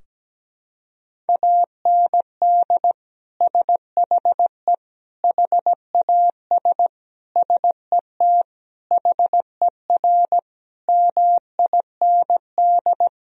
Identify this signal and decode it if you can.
Morse code, heard as a keyed tone, about 17 wpm, 700 Hz. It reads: AND SHE HAS SET HER MIND
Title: Sentences from Top 1000 Words - Review Part 4 - 17wpm